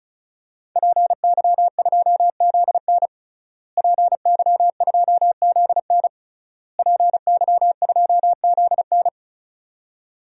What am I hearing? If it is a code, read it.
Morse code, heard as a keyed tone, about 34 wpm, 700 Hz. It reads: PY2ZN PY2ZN PY2ZN
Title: DXCC-100 Call Signs - 5 Characters - Repeated 3x - 35wpm